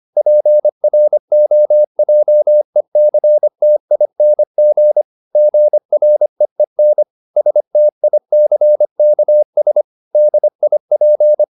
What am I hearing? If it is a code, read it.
Morse code, heard as a keyed tone, about 25 wpm, 600 Hz. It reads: PROJECTING GREEN STICKS DIP
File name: the old man and the sea part 1 25wpm